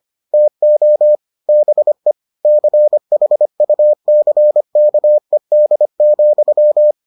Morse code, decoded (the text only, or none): TO BE CHUCKED,